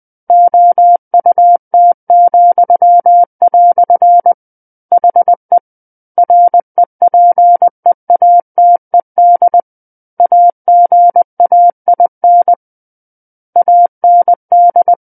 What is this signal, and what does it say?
Morse code, heard as a keyed tone, about 20 wpm, 700 Hz. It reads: OUT," HE REPEATED AGAIN AND
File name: 20 WPM